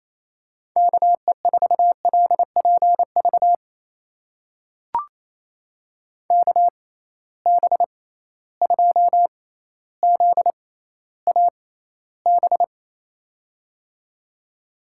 Morse code, heard as a keyed tone, about 28 wpm, spaced as for 14 wpm, 700 Hz. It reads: KE4LPV KB2ZAB
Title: USA Call Signs - 6 Characters - 6x Character Spacing - 28wpm